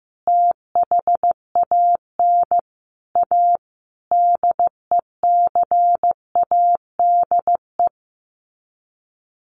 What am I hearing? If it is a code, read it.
Morse code, heard as a keyed tone, about 15 wpm, 700 Hz. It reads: THAN A DECADE
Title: Sets of 5 Words 15wpm